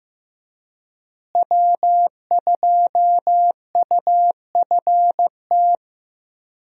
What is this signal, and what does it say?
Morse code, heard as a keyed tone, about 15 wpm, 700 Hz. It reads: W2UFT
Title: USA Call Signs - 5 Characters - Repeated 3x - 15wpm